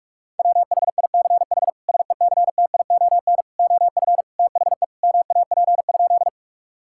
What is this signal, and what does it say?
Morse code, heard as a keyed tone, about 41 wpm, 700 Hz. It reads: WHICH SECTION OF THE MAP?
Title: Sentences from Top 1000 Words - Review Part 5 - 45wpm